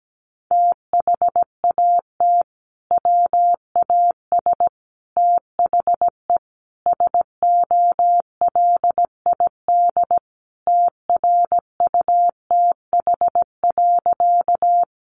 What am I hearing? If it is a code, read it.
Morse code, heard as a keyed tone, about 17 wpm, 700 Hz. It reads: THAT WAS THE SOLID TRUTH.
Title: Sentences from Top 1500 Words 17wpm